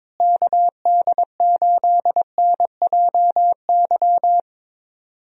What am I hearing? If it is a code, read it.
Morse code, heard as a keyed tone, about 22 wpm, 700 Hz. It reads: KD8NJY
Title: USA Call Signs - 6 Characters - Repeated 2x - 22wpm